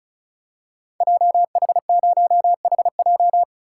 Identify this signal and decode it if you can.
Morse code, heard as a keyed tone, about 36 wpm, 700 Hz. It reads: JH0HJ
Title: Code Groups - 5 Letter:Numbers - 35wpm